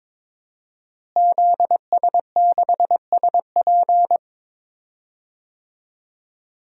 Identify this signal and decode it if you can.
Morse code, heard as a keyed tone, about 22 wpm, 700 Hz. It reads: ZS6SP
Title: DXCC-100 Call Signs - 5 Characters - Repeated 3x - 22wpm